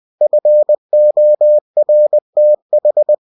Morse code, heard as a keyed tone, about 20 wpm, 600 Hz. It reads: FORTH
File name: cw-books / winnie-the-pooh / winnie-20wpm-page-0119